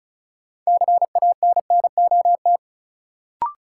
Morse code, heard as a keyed tone, about 36 wpm, 700 Hz. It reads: CANNOT
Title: Sentences from Top 1000 Words - Review Part 6 - New Word Drill - 35wpm